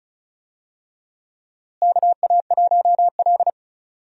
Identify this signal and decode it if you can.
Morse code, heard as a keyed tone, about 36 wpm, 700 Hz. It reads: KA1L
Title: USA Call Signs - 4 Characters - Repeated 2x - 35wpm